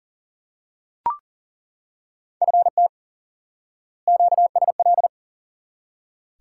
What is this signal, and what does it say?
Morse code, heard as a keyed tone, about 40 wpm, 700 Hz. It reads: FT QSL
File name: course / instant-qso-element-course-3 / IQE Course 3 - Lesson 89 - YAGI - Two Elements - 40wpm